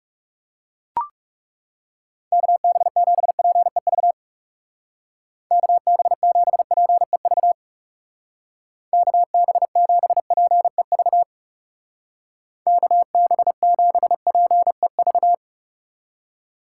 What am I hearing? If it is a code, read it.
Morse code, heard as a keyed tone, about 37 wpm, 700 Hz. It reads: KB7PEV KB7PEV KB7PEV KB7PEV
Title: Speed Racing - USA 2x3 Call Signs 30wpm